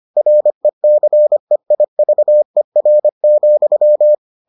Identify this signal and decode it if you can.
Morse code, heard as a keyed tone, about 25 wpm, 600 Hz. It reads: RECEIVER,